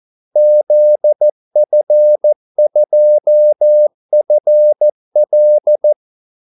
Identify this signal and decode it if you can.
Morse code, heard as a keyed tone, about 14 wpm, 600 Hz. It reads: ZF2FL
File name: Callsigns-14wpm